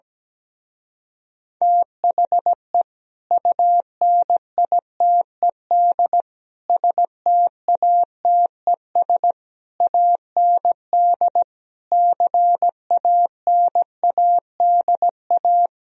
Morse code, heard as a keyed tone, about 17 wpm, 700 Hz. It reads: THE UNITED STATES AND CANADA
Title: Speed-Racing - Sets of 5 Words 17wpm